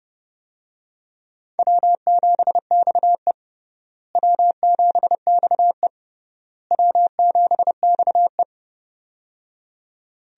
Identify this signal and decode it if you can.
Morse code, heard as a keyed tone, about 30 wpm, 700 Hz. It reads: W7XE W7XE W7XE
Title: USA Call Signs - 4 Characters - Repeated 3x - 30wpm